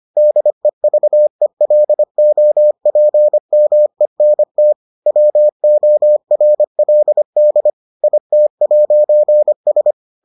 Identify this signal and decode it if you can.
Morse code, heard as a keyed tone, about 25 wpm, 600 Hz. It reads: DEVELOPMENT WORLD IT'S